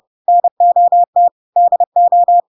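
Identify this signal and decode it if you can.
Morse code, heard as a keyed tone, about 30 wpm, 700 Hz. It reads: NOT DO
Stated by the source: Chapter 2 of The Adventures of Pinocchio read as a book in Morse code (CW) at 30 wpm, by ik1hge.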